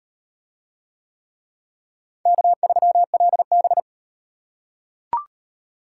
Morse code, heard as a keyed tone, about 35 wpm, 700 Hz. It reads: K3LB